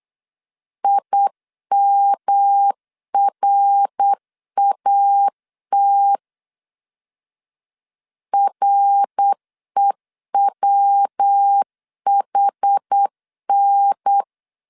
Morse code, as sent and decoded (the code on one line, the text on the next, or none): .. -- .-. .- - / .-. .-- .... -.
IMRAT REWHN